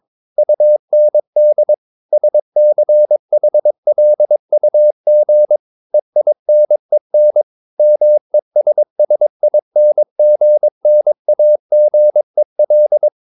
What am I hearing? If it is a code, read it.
Morse code, heard as a keyed tone, about 22 wpm, 600 Hz. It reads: UND SCHLUG EINEN MESSINGNAGEL